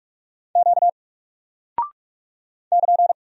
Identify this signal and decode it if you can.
Morse code, heard as a keyed tone, about 45 wpm, 700 Hz. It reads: Q <KN>
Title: Single Character (Letter:Number:Punctuation:Prosign) 45wpm